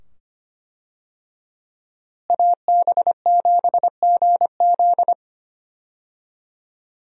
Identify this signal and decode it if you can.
Morse code, heard as a keyed tone, about 25 wpm, 700 Hz. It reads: AB7GZ
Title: USA Call Signs - 5 Characters - Repeated 2x - 25wpm